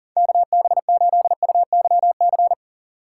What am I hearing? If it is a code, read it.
Morse code, heard as a keyed tone, about 37 wpm, 700 Hz. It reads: KB8UYC
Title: USA Call Signs - 4x Character Spacing - 40wpm